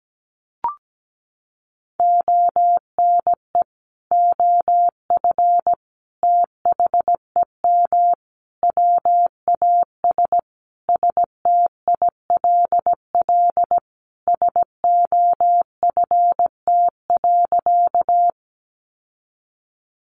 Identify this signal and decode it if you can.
Morse code, heard as a keyed tone, about 17 wpm, 700 Hz. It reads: ONE OF THEM WAS STILL SOFT.
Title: Sentences from Top 1500 Words 17wpm